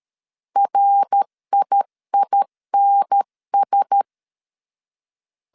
Morse code, heard as a keyed tone, about 13 wpm, 800 Hz. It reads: RIINS
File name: CW / DET / 1 6 1500C S C 12wpm 3533